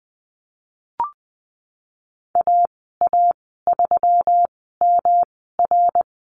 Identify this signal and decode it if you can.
Morse code, heard as a keyed tone, about 20 wpm, spaced as for 15 wpm, 700 Hz. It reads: AA3MR
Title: 128 Lesson 21g - Number 6 - Call Signs 20wpm